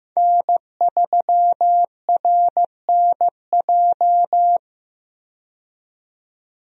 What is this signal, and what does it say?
Morse code, heard as a keyed tone, about 15 wpm, 700 Hz. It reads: N3RNJ
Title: USA Call Signs - 5 Characters - 15wpm